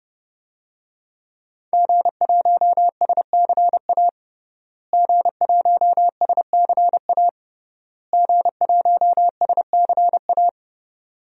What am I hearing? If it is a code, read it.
Morse code, heard as a keyed tone, about 32 wpm, 700 Hz. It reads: G1SCA G1SCA G1SCA
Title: DXCC-100 Call Signs - 5 Characters - Repeated 3x - 30wpm